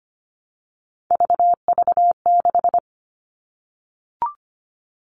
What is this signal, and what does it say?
Morse code, heard as a keyed tone, about 25 wpm, 700 Hz. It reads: VV6
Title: All Cut Numbers - 3-Digit 25wpm